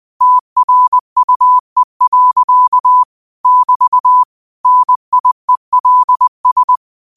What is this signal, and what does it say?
Morse code, heard as a keyed tone, about 20 wpm, 1000 Hz. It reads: TRUE. <BT> NIELS